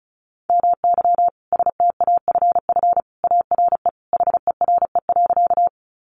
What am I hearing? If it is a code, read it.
Morse code, heard as a keyed tone, about 34 wpm, 700 Hz. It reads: MY STAFF ARE HERE.
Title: Sentences from Top 1200 Words 35wpm